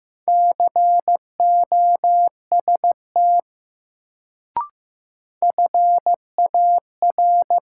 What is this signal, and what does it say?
Morse code, heard as a keyed tone, about 15 wpm, 700 Hz. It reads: COST FAR
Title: Sets of 3 Words - New Word Drill - 15wpm